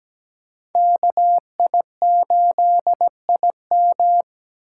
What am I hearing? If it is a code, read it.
Morse code, heard as a keyed tone, about 17 wpm, 700 Hz. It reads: KI8IM
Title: USA Call Signs - 5 Characters - 6x Character Spacing - 17wpm